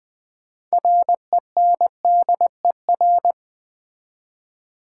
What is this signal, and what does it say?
Morse code, heard as a keyed tone, about 20 wpm, 700 Hz. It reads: RENDER
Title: 6 Letter Words - First 3 Letters Given - 20wpm